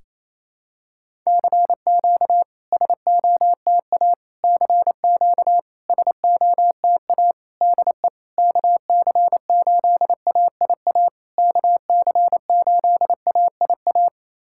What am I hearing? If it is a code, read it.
Morse code, heard as a keyed tone, about 27 wpm, 660 Hz. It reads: CQ SOTA CQ SOTA DE KC8AIA KC8AIA
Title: SOTA QSO Practice 28wpm